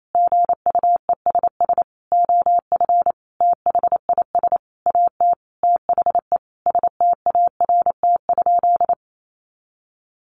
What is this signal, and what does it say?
Morse code, heard as a keyed tone, about 28 wpm, 700 Hz. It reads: GUESS OF THIS AT THE START?
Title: Sentences from Top 1500 Words 28wpm